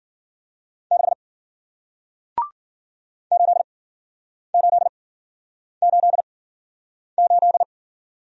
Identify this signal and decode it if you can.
Morse code, heard as a keyed tone, about 51 wpm, spaced as for 9 wpm, 700 Hz. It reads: 6 8888